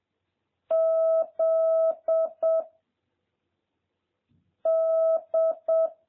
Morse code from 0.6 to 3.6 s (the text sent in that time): Z